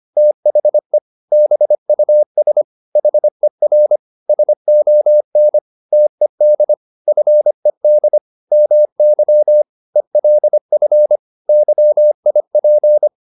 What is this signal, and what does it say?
Morse code, heard as a keyed tone, about 25 wpm, 600 Hz. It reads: THE BUS HER SON TED FED MY ELF YIP